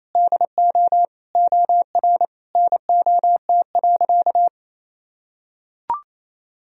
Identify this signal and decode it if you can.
Morse code, heard as a keyed tone, about 28 wpm, 700 Hz. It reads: DO OR NOT.
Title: Sentences from Top 1000 Words - Review Part 3 - 28wpm